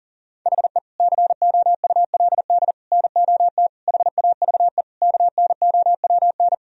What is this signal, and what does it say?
Morse code, heard as a keyed tone, about 40 wpm, 700 Hz. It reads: HE COULD NOT HAVE KNOWN
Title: Sentences from Top 1000 Words - Review Part 3 - 40wpm